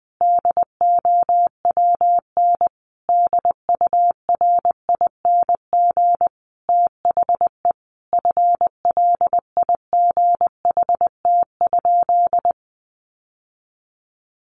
Morse code, heard as a keyed tone, about 20 wpm, 700 Hz. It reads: DOWN DURING THE FLIGHT?
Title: Sentences from Top 1000 Words 20wpm